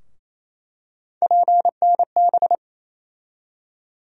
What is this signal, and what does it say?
Morse code, heard as a keyed tone, about 28 wpm, 700 Hz. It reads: PNB
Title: USA Call Sign Postfixes - 3 Letters 28wpm